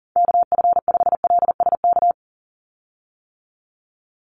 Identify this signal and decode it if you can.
Morse code, heard as a keyed tone, about 40 wpm, 700 Hz. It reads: KF5LSK
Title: USA Call Signs 40wpm